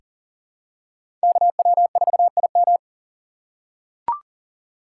Morse code, heard as a keyed tone, about 40 wpm, 700 Hz. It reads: KW4IM
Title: USA Call Signs - 5 Characters - 2x Character Spacing - 40wpm